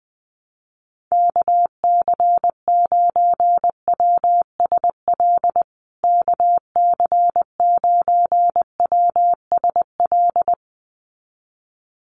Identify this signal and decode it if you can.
Morse code, heard as a keyed tone, about 20 wpm, 700 Hz. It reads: KC9WSL KC9WSL
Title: USA Call Signs Repeated 2x 20wpm